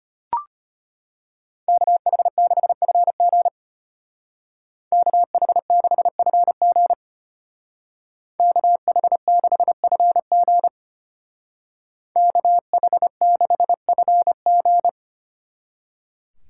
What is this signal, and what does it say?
Morse code, heard as a keyed tone, about 32 wpm, 700 Hz. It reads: KH6FG KH6FG KH6FG KH6FG